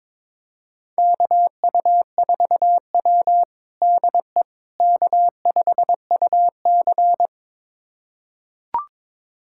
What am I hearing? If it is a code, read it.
Morse code, heard as a keyed tone, about 21 wpm, 700 Hz. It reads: KU4W DE K5UC
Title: USA Callsign DE Callsign - 4 Characters - 22wpm